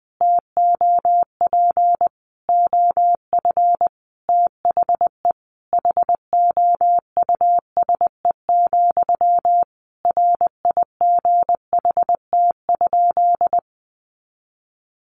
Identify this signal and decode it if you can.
Morse code, heard as a keyed tone, about 19 wpm, 700 Hz. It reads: TOP OF THE HOUSE, RIGHT?
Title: Sentences from Top 1200 Words 20wpm